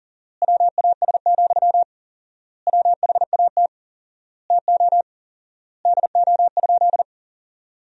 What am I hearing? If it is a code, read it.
Morse code, heard as a keyed tone, about 37 wpm, 700 Hz. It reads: WAS, WHAT TO DO?
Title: 4x Word-Spacing - Sentences from Top 400 Words - 40wpm